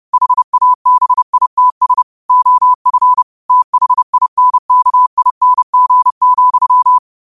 Morse code, heard as a keyed tone, about 30 wpm, 1000 Hz. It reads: HABITS OF THINKING,